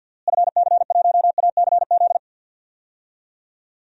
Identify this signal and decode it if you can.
Morse code, heard as a keyed tone, about 47 wpm, 700 Hz. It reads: FC1ACZ
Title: DXCC-100 Call Signs - 6 Characters - Repeated 2x - 50wpm